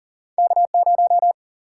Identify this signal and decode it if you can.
Morse code, heard as a keyed tone, about 39 wpm, 700 Hz. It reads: K0
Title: Single-Letter Single-Number 40wpm